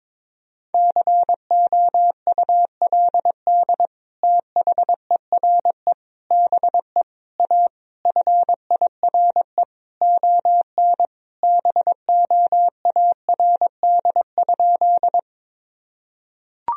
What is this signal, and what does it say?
Morse code, heard as a keyed tone, about 22 wpm, 700 Hz. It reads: COULD THERE BE A FIRE ON BOARD?
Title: Sentences from Top 1000 Words - Review Part 4 - 22wpm